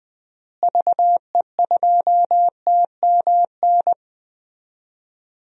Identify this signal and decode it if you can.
Morse code, heard as a keyed tone, about 20 wpm, 700 Hz. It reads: VE2TMN
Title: DXCC-100 Call Signs - 6 Characters - Repeated 2x - 20wpm